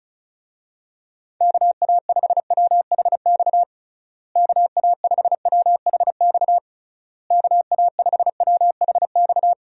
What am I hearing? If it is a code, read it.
Morse code, heard as a keyed tone, about 34 wpm, 700 Hz. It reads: KA5WHX KA5WHX KA5WHX
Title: USA Call Signs - 6 Characters - Repeated 3x - 35wpm